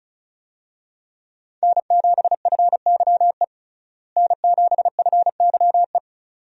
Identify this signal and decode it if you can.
Morse code, heard as a keyed tone, about 36 wpm, 700 Hz. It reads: N7FYE N7FYE
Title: USA Call Signs - 5 Characters - Repeated 2x - 35wpm